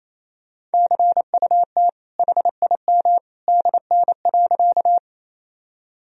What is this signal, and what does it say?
Morse code, heard as a keyed tone, about 28 wpm, 700 Hz. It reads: CUT HIM DN.
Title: Sentences from Top 300 Words and Abbreviations 28wpm